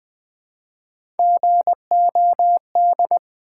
Morse code, heard as a keyed tone, about 20 wpm, 700 Hz. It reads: GOD
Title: Build A Word - 1 Letter At A Time - 3-Letter Words - 20wpm